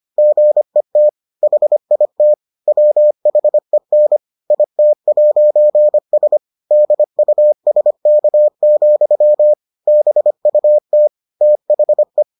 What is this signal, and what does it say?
Morse code, heard as a keyed tone, about 25 wpm, 600 Hz. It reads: GET HIT WHEN IT'S DUSK, BUT THE